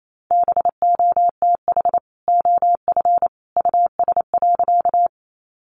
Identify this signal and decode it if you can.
Morse code, heard as a keyed tone, about 28 wpm, 700 Hz. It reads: BOTH OF US.
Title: Sentences from Top 300 Words 28wpm